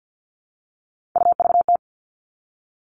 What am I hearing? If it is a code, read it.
Morse code, heard as a keyed tone, about 50 wpm, 700 Hz. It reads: UVT
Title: All Cut Numbers - 3-Digit 50wpm